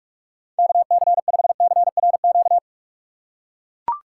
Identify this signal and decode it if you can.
Morse code, heard as a keyed tone, about 43 wpm, 700 Hz. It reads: KC5CRQ
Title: USA Call Signs - 6 Characters - 3x Character Spacing - 45wpm